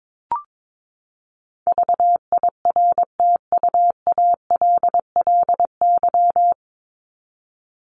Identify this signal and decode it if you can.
Morse code, heard as a keyed tone, about 22 wpm, 700 Hz. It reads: VIRTUALLY
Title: LY Suffix - Top 100 Words with Base Words 22wpm